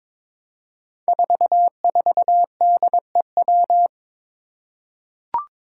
Morse code, heard as a keyed tone, about 22 wpm, 700 Hz. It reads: 44DEW
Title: Five Challenging Letter-Number Combinations 22wpm